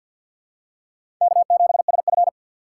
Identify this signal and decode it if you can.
Morse code, heard as a keyed tone, about 51 wpm, 700 Hz. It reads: K7SF
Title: K1USN SST QSO Practice - 50wpm